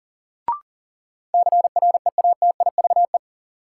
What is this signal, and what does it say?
Morse code, heard as a keyed tone, about 40 wpm, 700 Hz. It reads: CREATIVE
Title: Sentences from Top 1000 Words - Review Part 5 - New Word Drill - 40wpm